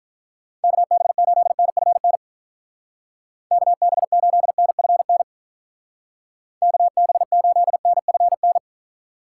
Morse code, heard as a keyed tone, about 49 wpm, 700 Hz. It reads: KB8NFN KB8NFN KB8NFN